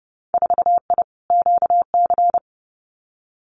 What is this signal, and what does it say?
Morse code, heard as a keyed tone, about 32 wpm, 700 Hz. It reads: 4I QC
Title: Winter Field Day Exchanges Encore 30wpm